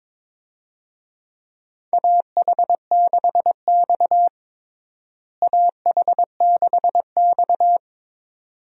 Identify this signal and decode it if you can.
Morse code, heard as a keyed tone, about 22 wpm, 700 Hz. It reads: AH6X AH6X
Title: USA Call Signs - 4 Characters - Repeated 2x - 22wpm